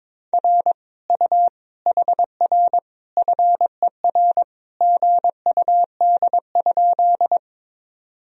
Morse code, heard as a keyed tone, about 23 wpm, 700 Hz. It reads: R U HR FER GUD?
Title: Sentences from Top 200 Words and Abbreviations 22wpm